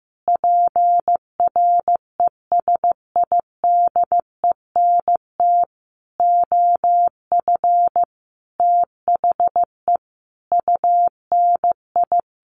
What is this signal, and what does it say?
Morse code, heard as a keyed tone, about 15 wpm, 700 Hz. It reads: PRESIDENT OF THE UNI
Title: Sets of 5 Words Encore2 15wpm